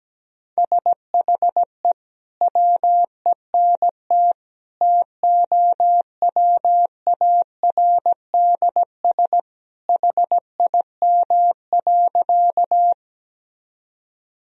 Morse code, heard as a keyed tone, about 17 wpm, 700 Hz. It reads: SHE WENT TOWARDS HIM.
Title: Sentences from Top 1000 Words - Review Part 2 - 17wpm